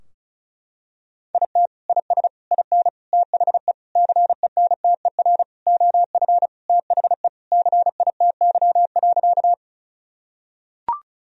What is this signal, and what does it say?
Morse code, heard as a keyed tone, about 36 wpm, 700 Hz. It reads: IT IS IN THE CENTER OF THE CITY.
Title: Sentences from Top 1000 Words - Review Part 2 - 35wpm